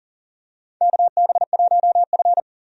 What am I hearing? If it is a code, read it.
Morse code, heard as a keyed tone, about 38 wpm, 700 Hz. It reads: KB1F
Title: USA Call Signs - 4 Characters - 40wpm